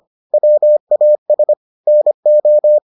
Morse code, heard as a keyed tone, about 25 wpm, 600 Hz. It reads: WAS NO